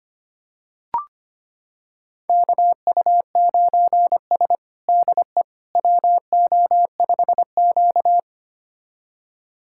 Morse code, heard as a keyed tone, about 25 wpm, 700 Hz. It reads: KU9S DE WO5Q
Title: USA Callsign DE Callsign - 4 Characters - 25wpm